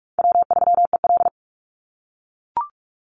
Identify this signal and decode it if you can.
Morse code, heard as a keyed tone, about 44 wpm, 700 Hz. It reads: W3EL
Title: Call Signs - 45wpm